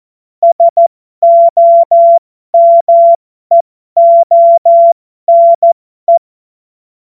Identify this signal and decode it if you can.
Morse code, heard as a keyed tone, about 14 wpm, spaced as for 12 wpm, 680 Hz. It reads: SOMEONE
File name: Morsekurs / Witze